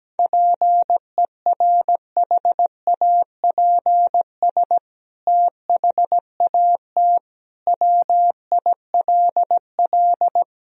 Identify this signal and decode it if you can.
Morse code, heard as a keyed tone, about 17 wpm, 700 Hz. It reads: PERHAPS THAT WILL
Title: Sentences from Top 1000 Words - Review Part 2 - 17wpm